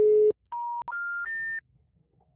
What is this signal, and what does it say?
Special information tone.
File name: out- -2021- - - .37348